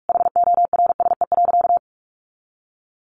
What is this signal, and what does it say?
Morse code, heard as a keyed tone, about 44 wpm, 700 Hz. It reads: HORSE.
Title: Sentences from Top 1500 Words 45wpm